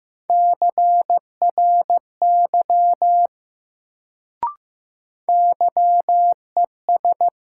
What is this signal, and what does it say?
Morse code, heard as a keyed tone, about 15 wpm, 700 Hz. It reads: CRY YES
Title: Sentences from Top 500 Words - New Word Drill - 15wpm